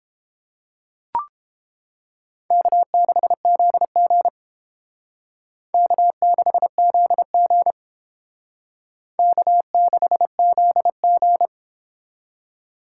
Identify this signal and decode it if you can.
Morse code, heard as a keyed tone, about 30 wpm, 700 Hz. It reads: K6ZG K6ZG K6ZG